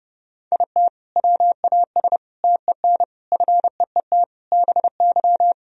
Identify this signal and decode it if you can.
Morse code, heard as a keyed tone, about 29 wpm, 700 Hz. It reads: IT WAS TEN FEET BY
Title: Sentences from Top 1000 Words - Review Part 2 - 30wpm